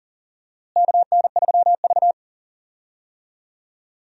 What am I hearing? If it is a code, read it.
Morse code, heard as a keyed tone, about 40 wpm, 700 Hz. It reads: KN3V
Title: USA Call Signs - 4 Characters - Repeated 3x - 40wpm